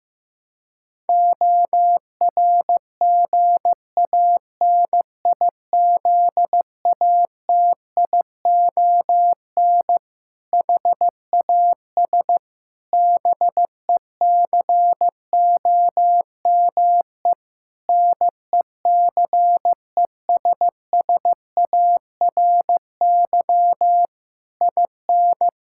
Morse code, heard as a keyed tone, about 15 wpm, 700 Hz. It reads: ORGANIZATION HAS BECOME NECESSARY IN
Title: Sentences from Top 1000 Words - Review Part 5 - 15wpm